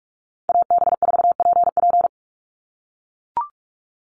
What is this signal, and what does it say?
Morse code, heard as a keyed tone, about 44 wpm, 690 Hz. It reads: AB4PP
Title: NA SOTA Chaser Callsigns 45wpm